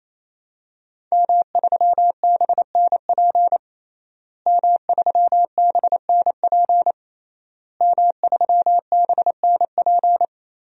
Morse code, heard as a keyed tone, about 27 wpm, 700 Hz. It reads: M3BNP M3BNP M3BNP
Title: DXCC-100 Call Signs - 5 Characters - Repeated 3x - 28wpm